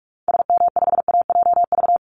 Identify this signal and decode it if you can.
Morse code, heard as a keyed tone, about 44 wpm, 700 Hz. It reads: SM5AJV
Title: Call Signs Repeated 3x - 45wpm